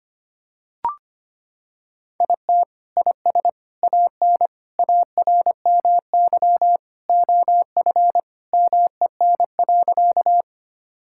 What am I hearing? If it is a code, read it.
Morse code, heard as a keyed tone, about 25 wpm, 700 Hz. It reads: IT IS AN ARMY OF MEN.
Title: Sentences from Top 1000 Words - Review Part 5 - 25wpm